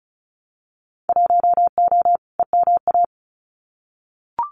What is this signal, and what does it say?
Morse code, heard as a keyed tone, about 36 wpm, 700 Hz. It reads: 1O EMA
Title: Winter Field Day Exchanges Encore 35wpm